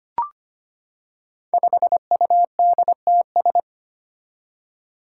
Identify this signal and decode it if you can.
Morse code, heard as a keyed tone, about 25 wpm, 700 Hz. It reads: 5UDTS
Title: Five Challenging Letter-Number Combinations 25wpm